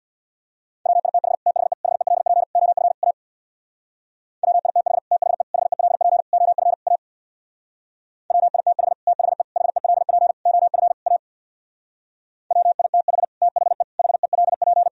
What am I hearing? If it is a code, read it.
Morse code, heard as a keyed tone, about 64 wpm, 690 Hz. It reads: WITH THE HELP OF A WITH THE HELP OF A WITH THE HELP OF A WITH THE HELP